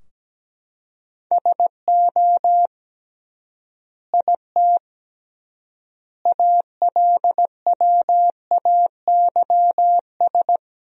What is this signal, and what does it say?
Morse code, heard as a keyed tone, about 17 wpm, 700 Hz. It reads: SO IT ALWAYS
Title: 3x Word-Spacing - Sentences from Top 300 Words - 17wpm